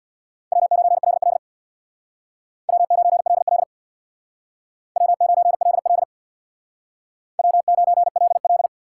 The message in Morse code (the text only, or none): W9LL W9LL W9LL W9LL